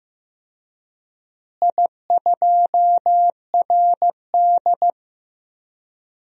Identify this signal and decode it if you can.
Morse code, heard as a keyed tone, about 15 wpm, 700 Hz. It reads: I2RD
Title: DXCC-100 Call Signs - 4 Characters - Repeated 3x - 15wpm